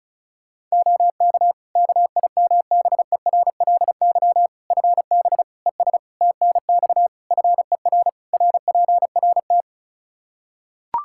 Morse code, heard as a keyed tone, about 36 wpm, 700 Hz. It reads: OK KIMBERLY FB ES TNX FER RPRT
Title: Practice QSO - Exchange 2 - Info - 35wpm